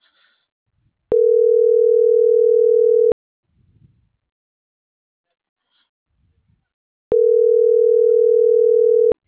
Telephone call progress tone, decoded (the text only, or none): ringback tone